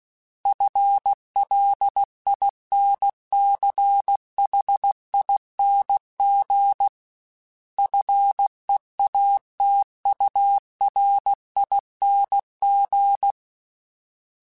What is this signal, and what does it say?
Morse code, heard as a keyed tone, about 16 wpm, 800 Hz. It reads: FLINCHING FEATURING
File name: Special INGs / Special INGs-mp3s / Sngl ing @15